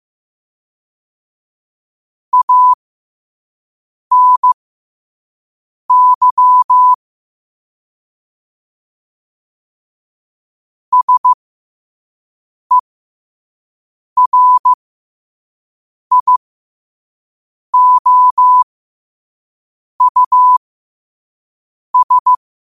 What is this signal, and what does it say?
Morse code, heard as a keyed tone, about 15 wpm, spaced as for 5 wpm, 1000 Hz. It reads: ANY SERIOUS
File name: qotdfiles / 5wpm